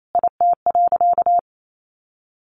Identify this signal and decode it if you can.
Morse code, heard as a keyed tone, about 28 wpm, 700 Hz. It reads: IT.
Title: Sentences from Top 1200 Words 28wpm